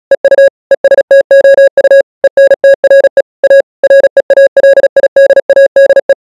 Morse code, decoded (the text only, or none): EU ESTOU ENTRE A REALIDADE